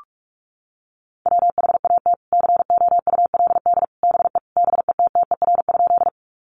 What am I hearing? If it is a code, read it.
Morse code, heard as a keyed tone, about 44 wpm, 700 Hz. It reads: WHAT COULD BE BETTER?
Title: Sentences from Top 400 Words 45wpm